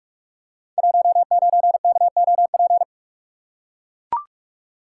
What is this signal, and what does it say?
Morse code, heard as a keyed tone, about 44 wpm, 700 Hz. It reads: 19KOP
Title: Code Groups - 5 Letter:Numbers - 45wpm